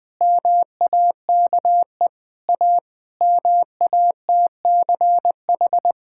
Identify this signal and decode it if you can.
Morse code, heard as a keyed tone, about 20 wpm, 700 Hz. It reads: MAKE A MATCH
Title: Sentences from Top 1600 Words 20wpm